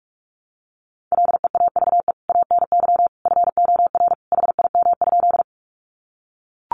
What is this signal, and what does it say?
Morse code, heard as a keyed tone, about 44 wpm, 700 Hz. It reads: LEAVE ANY FOR HIM?
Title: Sentences from Top 400 Words 45wpm